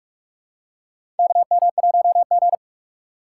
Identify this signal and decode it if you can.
Morse code, heard as a keyed tone, about 42 wpm, 700 Hz. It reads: KM1G
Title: USA Call Signs - 2x Character Spacing - 45wpm